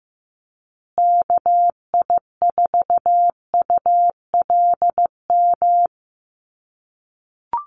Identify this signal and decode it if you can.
Morse code, heard as a keyed tone, about 15 wpm, 700 Hz. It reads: KI4ULM
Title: USA Call Signs Repeated 3x 15wpm